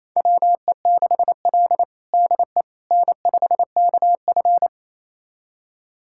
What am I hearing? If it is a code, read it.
Morse code, heard as a keyed tone, about 28 wpm, 700 Hz. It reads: WE6L DE N5KF
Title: USA Callsign DE Callsign - 4 Characters - 28wpm